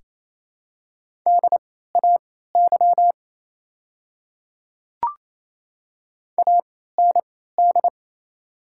Morse code, heard as a keyed tone, about 29 wpm, spaced as for 16 wpm, 700 Hz. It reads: DAY AND
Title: 3x Character Spacing - Top 100 Words - 28wpm